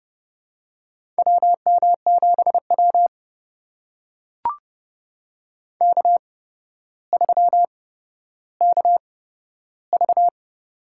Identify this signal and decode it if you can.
Morse code, heard as a keyed tone, about 30 wpm, spaced as for 13 wpm, 700 Hz. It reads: WM7W K3KV
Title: USA Call Signs - 4 Characters - 8x Character Spacing - 30wpm